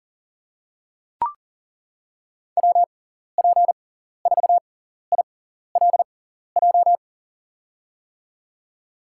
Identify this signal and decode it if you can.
Morse code, heard as a keyed tone, about 40 wpm, spaced as for 14 wpm, 700 Hz. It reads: WP4ILJ
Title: USA Call Signs - 6x Character Spacing - 40wpm